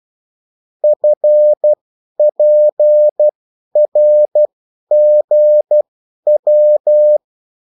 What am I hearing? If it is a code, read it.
Morse code, heard as a keyed tone, about 12 wpm, spaced as for 10 wpm, 600 Hz. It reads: FPRGW
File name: morse mp3 / revision9-vitesse 10 mots